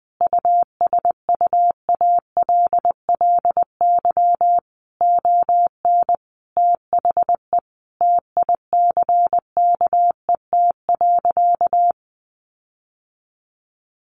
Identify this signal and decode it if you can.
Morse code, heard as a keyed tone, about 20 wpm, 700 Hz. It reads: USUALLY ON THE TICKET.
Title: Sentences from Top 1400 Words 20wpm